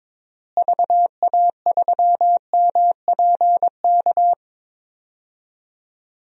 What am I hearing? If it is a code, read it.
Morse code, heard as a keyed tone, about 22 wpm, 700 Hz. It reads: VA3MPK
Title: DXCC-100 Call Signs - 6 Characters - Repeated 2x - 22wpm